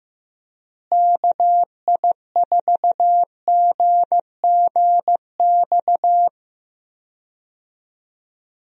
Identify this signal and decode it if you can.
Morse code, heard as a keyed tone, about 15 wpm, 700 Hz. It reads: KI4GGX
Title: USA Call Signs - 6 Characters - Repeated 3x - 15wpm